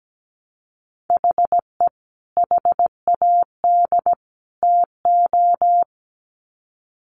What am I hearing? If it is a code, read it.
Morse code, heard as a keyed tone, about 17 wpm, 700 Hz. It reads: HE HAD TO
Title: Sets of 3 Words 17wpm